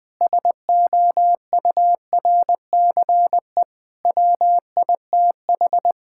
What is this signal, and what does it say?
Morse code, heard as a keyed tone, about 20 wpm, 700 Hz. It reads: SOURCE WITH
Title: Sentences with 9-Word Phrases - Spoken Beginning - 20wpm